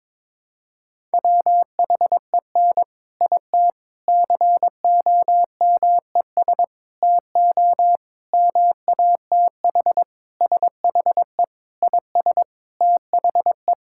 Morse code, heard as a keyed tone, about 22 wpm, 700 Hz. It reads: WHEN IT COMES TO MATH SHE IS THE
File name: practice / 22wpm / Sentences with 4-Word Phrases - 22wpm